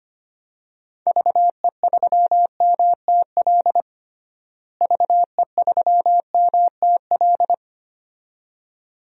Morse code, heard as a keyed tone, about 25 wpm, 700 Hz. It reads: VE3MTL VE3MTL
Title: DXCC-100 Call Signs - 6 Characters - Repeated 2x - 25wpm